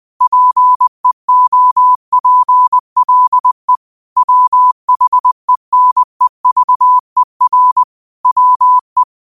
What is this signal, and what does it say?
Morse code, heard as a keyed tone, about 20 wpm, 1000 Hz. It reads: PEOPLE WHENEVER WE